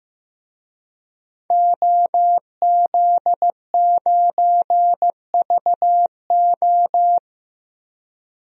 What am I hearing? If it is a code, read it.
Morse code, heard as a keyed tone, about 15 wpm, 700 Hz. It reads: OZ9VO
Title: DXCC-100 Call Signs - 5 Characters - Repeated 2x - 15wpm